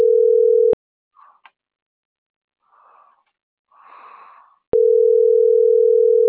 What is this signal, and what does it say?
Ringback tone (North America), 440 + 480 Hz.
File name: CENTRAL MABE / out edwinomarojeda - .24960